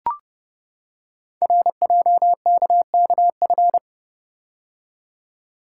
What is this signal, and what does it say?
Morse code, heard as a keyed tone, about 30 wpm, 690 Hz. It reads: RJKKF
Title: Code Groups - 5 Letters - 30wpm